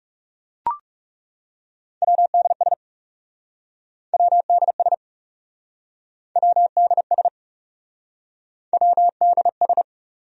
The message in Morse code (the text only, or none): WDS WDS WDS WDS